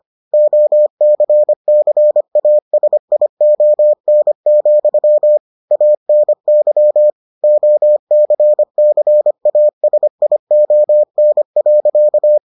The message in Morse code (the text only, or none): OCCASION, ANY OCCASION.